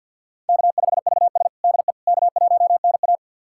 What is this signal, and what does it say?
Morse code, heard as a keyed tone, about 51 wpm, 700 Hz. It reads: K5VS DE K1NA
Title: USA Callsign DE Callsign - 4 Characters - 50wpm